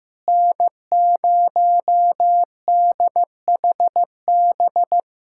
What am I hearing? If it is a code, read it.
Morse code, heard as a keyed tone, about 15 wpm, 700 Hz. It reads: N0DHB
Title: USA Callsign DE Callsign - 5 Characters - 15wpm